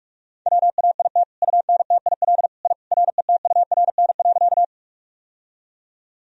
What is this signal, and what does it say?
Morse code, heard as a keyed tone, about 44 wpm, 700 Hz. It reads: WAIT UNTIL I RETURN.
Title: Sentences from Top 1000 Words - Review Part 1 - 45wpm